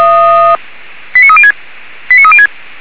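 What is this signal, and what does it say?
Five-tone selective call (ZVEI), heard as quick runs of five tones, tones of about 70 ms. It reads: 89306 89306
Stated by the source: ZVEI five-tone selective call received with an SDR.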